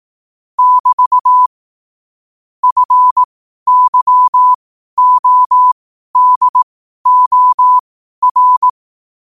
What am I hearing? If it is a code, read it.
Morse code, heard as a keyed tone, about 18 wpm, spaced as for 12 wpm, 1000 Hz. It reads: <BT> FYODOR
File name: qotdfiles / 12wpm